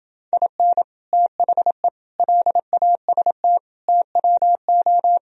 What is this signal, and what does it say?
Morse code, heard as a keyed tone, about 28 wpm, 700 Hz. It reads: IN THE LAST TWO